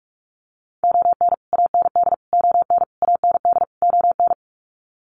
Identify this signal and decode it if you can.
Morse code, heard as a keyed tone, about 44 wpm, 690 Hz. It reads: ON AND ON AND ON
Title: Sets of 5 Words Encore2 45wpm